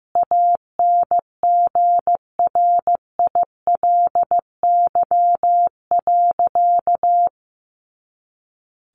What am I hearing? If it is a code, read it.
Morse code, heard as a keyed tone, about 15 wpm, 700 Hz. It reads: ANGRILY.
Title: A Sound Of Thunder 15wpm